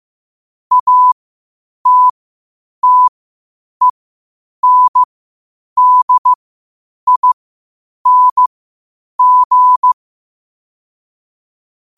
Morse code, heard as a keyed tone, about 15 wpm, spaced as for 8 wpm, 1000 Hz. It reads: ATTENDING
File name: qotdfiles / 8wpm